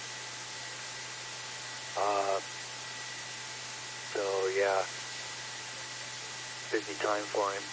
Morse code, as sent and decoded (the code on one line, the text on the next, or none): none